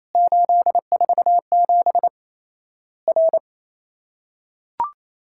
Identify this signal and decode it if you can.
Morse code, heard as a keyed tone, about 28 wpm, 700 Hz. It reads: 847 R
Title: Worked All Europe DX Contest - QTC Exchange - 28wpm